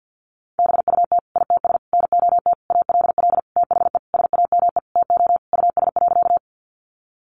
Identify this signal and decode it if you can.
Morse code, heard as a keyed tone, about 50 wpm, 700 Hz. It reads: BUT ITS NOT ALL THE SAME TO US.
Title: Sentences from Top 600 Words 50wpm